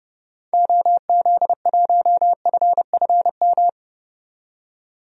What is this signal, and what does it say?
Morse code, heard as a keyed tone, about 31 wpm, 690 Hz. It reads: OZ1FFM
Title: DXCC-100 Call Signs - 6 Characters - 30wpm